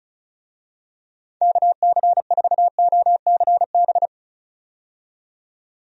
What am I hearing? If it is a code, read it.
Morse code, heard as a keyed tone, about 36 wpm, 700 Hz. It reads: KC4OCB
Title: USA Call Signs - 6 Characters - 35wpm